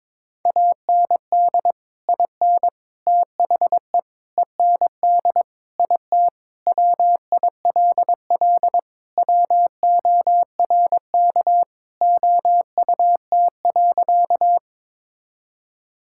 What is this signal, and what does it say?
Morse code, heard as a keyed tone, about 22 wpm, 700 Hz. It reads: AND IN THE END IT WILL WORK OUT.
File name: practice / 22wpm / Sentences with 3-Word Phrases - 22wpm